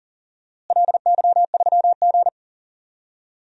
Morse code, heard as a keyed tone, about 37 wpm, 690 Hz. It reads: LY3G
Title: DXCC-100 Call Signs - 4 Characters - Repeated 2x - 40wpm